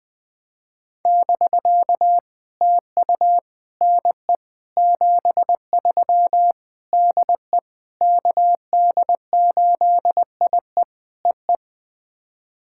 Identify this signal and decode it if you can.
Morse code, heard as a keyed tone, about 20 wpm, 700 Hz. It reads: <BK> TU NE 73 DE KD8IE EE
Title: POTA QSO Practice - 20wpm